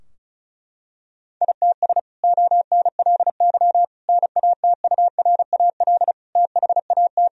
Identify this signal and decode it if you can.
Morse code, heard as a keyed tone, about 34 wpm, 700 Hz. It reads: ITS ONLY NATURAL THAT
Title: Sentences from Top 1000 Words - Review Part 4 - 35wpm